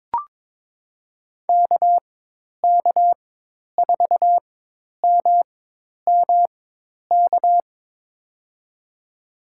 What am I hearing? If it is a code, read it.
Morse code, heard as a keyed tone, about 23 wpm, spaced as for 10 wpm, 700 Hz. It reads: KK4MMK
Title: USA Call Signs - 4x Character Spacing - 22wpm